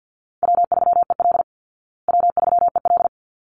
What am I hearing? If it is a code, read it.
Morse code, heard as a keyed tone, about 50 wpm, 700 Hz. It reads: W3EL W3EL
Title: Easy Call Signs Repeated 2x - 50wpm